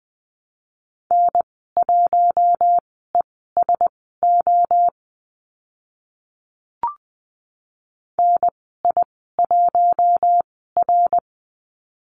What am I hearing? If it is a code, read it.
Morse code, heard as a keyed tone, about 20 wpm, spaced as for 15 wpm, 700 Hz. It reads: N1ESO NI1R
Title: 055 Lesson 10g - Letter r - Call Signs 20wpm